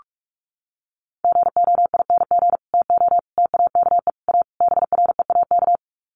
Morse code, heard as a keyed tone, about 44 wpm, 700 Hz. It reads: GOING TO TAKE A BREAK
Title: Sets of 5 Words 45wpm